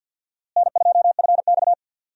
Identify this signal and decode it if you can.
Morse code, heard as a keyed tone, about 53 wpm, 700 Hz. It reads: N2FX
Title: USA Call Signs - 4 Characters - Repeated 2x - 50wpm